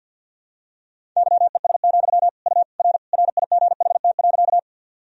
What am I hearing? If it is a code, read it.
Morse code, heard as a keyed tone, about 49 wpm, 700 Hz. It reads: YES, U R RIGHT.
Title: Sentences from Top 500 Words and Abbreviations 50wpm